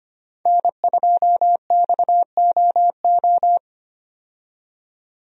Morse code, heard as a keyed tone, about 25 wpm, 700 Hz. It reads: N2XOO
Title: USA Call Signs - 5 Characters - 25wpm